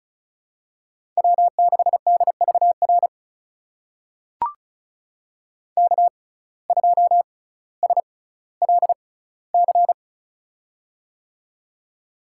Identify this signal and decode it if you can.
Morse code, heard as a keyed tone, about 34 wpm, spaced as for 18 wpm, 700 Hz. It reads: W6DVR K2SLC
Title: USA Call Signs - 5 Characters - 6x Character Spacing - 35wpm